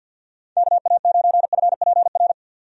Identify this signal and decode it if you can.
Morse code, heard as a keyed tone, about 52 wpm, 700 Hz. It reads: KA9FPR
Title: USA Call Signs - 6 Characters - 50wpm